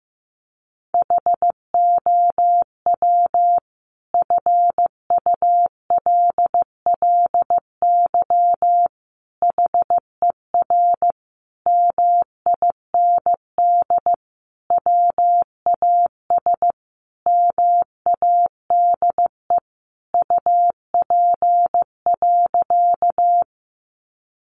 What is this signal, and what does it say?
Morse code, heard as a keyed tone, about 15 wpm, 700 Hz. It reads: HOW FULLY HER MIND WAS MADE UP.
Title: Sentences from Top 1300 Words 15wpm